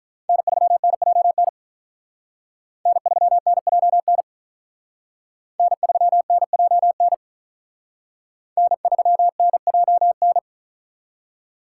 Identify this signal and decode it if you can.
Morse code, heard as a keyed tone, about 44 wpm, 700 Hz. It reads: N3NJN N3NJN N3NJN N3NJN